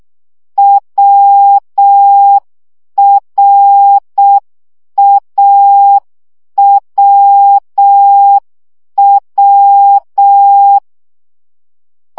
Morse code, heard as a keyed tone, about 6 wpm, 800 Hz. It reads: WRAWW
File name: CW gyakorlas a r w 5 baud